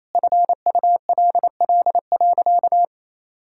Morse code, heard as a keyed tone, about 28 wpm, 700 Hz. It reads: FULL.
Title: Sentences from Top 1600 Words 28wpm